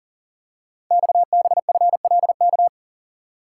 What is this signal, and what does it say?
Morse code, heard as a keyed tone, about 42 wpm, 700 Hz. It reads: XBFLK
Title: Code Groups - 5 Letters - 40wpm